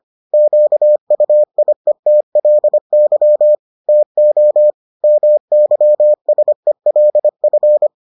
Morse code, heard as a keyed tone, about 25 wpm, 600 Hz. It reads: QUIETLY TO MYSELF